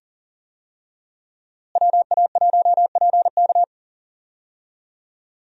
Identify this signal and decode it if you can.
Morse code, heard as a keyed tone, about 40 wpm, 700 Hz. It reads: WA1PK
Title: USA Call Signs - 5 Characters - Repeated 2x - 40wpm